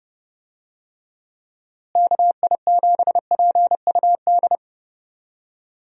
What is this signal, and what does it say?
Morse code, heard as a keyed tone, about 30 wpm, 700 Hz. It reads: KI7PUD
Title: USA Call Signs - 6 Characters - Repeated 2x - 30wpm